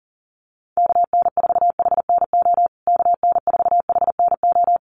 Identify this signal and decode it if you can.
Morse code, heard as a keyed tone, about 37 wpm, 700 Hz. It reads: KN4HNO KN4HNO
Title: USA Call Signs Repeated 3x 40wpm